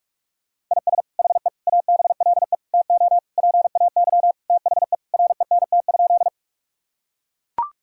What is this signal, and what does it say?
Morse code, heard as a keyed tone, about 41 wpm, 700 Hz. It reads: IS HE ABLE TO PAY THE RENT?
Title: Sentences from Top 1000 Words - Review Part 5 - 45wpm